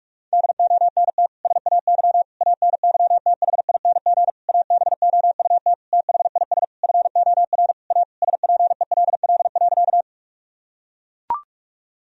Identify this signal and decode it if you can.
Morse code, heard as a keyed tone, about 41 wpm, 700 Hz. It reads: DONT SAY ANYTHING ABOUT THIS FOR A SPELL.
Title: Sentences from Top 1000 Words - Review Part 1 - 45wpm